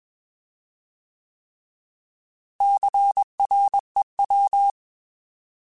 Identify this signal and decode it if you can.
Morse code, heard as a keyed tone, about 21 wpm, 800 Hz. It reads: CREW